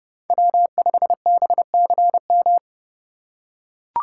Morse code, heard as a keyed tone, about 30 wpm, 700 Hz. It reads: W5BCM